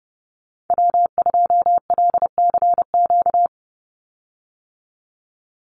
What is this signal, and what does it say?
Morse code, heard as a keyed tone, about 30 wpm, 700 Hz. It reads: W2LCQ
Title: Call Signs Repeated 3x - 30wpm